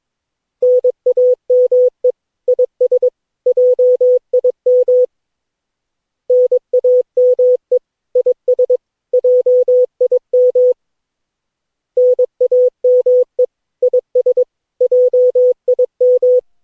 Morse code, heard as a keyed tone, about 22 wpm, 500 Hz. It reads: NAME IS JIM NAME IS JIM NAME IS JIM
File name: cw / Session9 cw training